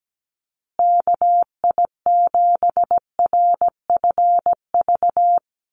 Morse code, heard as a keyed tone, about 17 wpm, 700 Hz. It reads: KI7RFV
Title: USA Call Signs Repeated 2x 17wpm